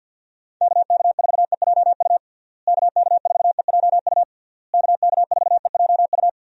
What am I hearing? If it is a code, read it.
Morse code, heard as a keyed tone, about 46 wpm, 700 Hz. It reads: KK4EJU KK4EJU KK4EJU
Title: USA Call Signs - 6 Characters - Repeated 3x - 50wpm